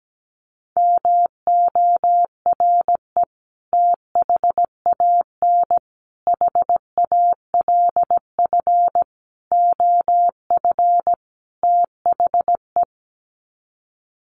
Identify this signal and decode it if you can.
Morse code, heard as a keyed tone, about 17 wpm, 700 Hz. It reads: MORE THAN HALF OF THE
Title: Sets of 5 Words 17wpm